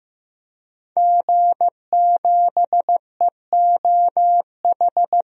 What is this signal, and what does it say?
Morse code, heard as a keyed tone, about 15 wpm, 700 Hz. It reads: G7EOH
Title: DXCC-100 Call Signs - 5 Characters - 15wpm